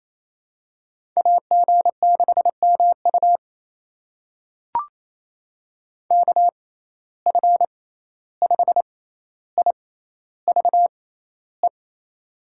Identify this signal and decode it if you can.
Morse code, heard as a keyed tone, about 28 wpm, spaced as for 14 wpm, 700 Hz. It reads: AG6MU KF5IVE
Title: USA Call Signs - 6x Character Spacing - 28wpm